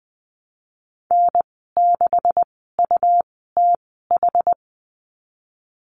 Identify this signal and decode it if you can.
Morse code, heard as a keyed tone, about 20 wpm, spaced as for 15 wpm, 700 Hz. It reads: N6UTH
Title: 186 Lesson 30g - Letter b - Call Signs 20wpm